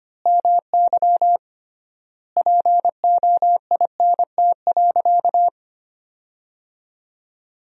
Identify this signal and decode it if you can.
Morse code, heard as a keyed tone, about 25 wpm, 700 Hz. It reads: MY POINT.
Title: 3x Word-Spacing - Sentences from Top 200 Words - 25wpm